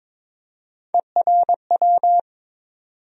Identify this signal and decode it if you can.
Morse code, heard as a keyed tone, about 22 wpm, 700 Hz. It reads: ERW
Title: USA Call Sign Postfixes - 3 Letters 22wpm